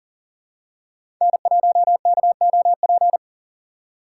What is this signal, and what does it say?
Morse code, heard as a keyed tone, about 40 wpm, 700 Hz. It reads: N1KOP
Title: USA Call Signs - 4x Character Spacing - 40wpm